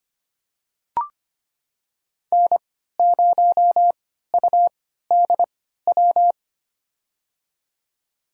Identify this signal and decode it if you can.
Morse code, heard as a keyed tone, about 25 wpm, spaced as for 14 wpm, 700 Hz. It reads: N0UDW
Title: USA Call Signs - 5 Characters - 3x Character Spacing - 25wpm